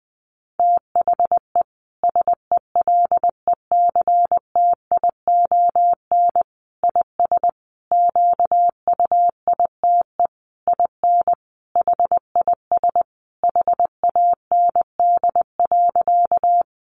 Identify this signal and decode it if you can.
Morse code, heard as a keyed tone, about 20 wpm, 700 Hz. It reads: THE SELECTION IS QUITE IN HIS HAND.
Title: Sentences from Top 1400 Words 20wpm